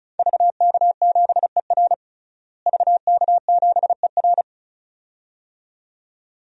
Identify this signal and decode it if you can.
Morse code, heard as a keyed tone, about 36 wpm, 700 Hz. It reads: VK7ER VK7ER
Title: DXCC-100 Call Signs - 5 Characters - Repeated 2x - 35wpm